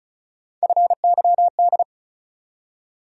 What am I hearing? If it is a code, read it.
Morse code, heard as a keyed tone, about 36 wpm, 700 Hz. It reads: FYD
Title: USA Call Sign Postfixes - 3 Letters 35wpm